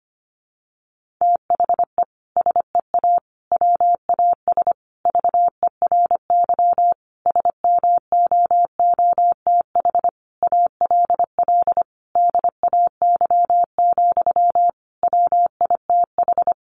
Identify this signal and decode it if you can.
Morse code, heard as a keyed tone, about 25 wpm, 700 Hz. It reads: THE SEA WAS VERY SMOOTH ALL DAY, WITH
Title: Sentences from Top 1400 Words 25wpm